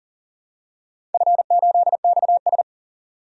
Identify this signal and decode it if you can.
Morse code, heard as a keyed tone, about 41 wpm, 700 Hz. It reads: F8XS
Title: DXCC-100 Call Signs - 4 Characters - 40wpm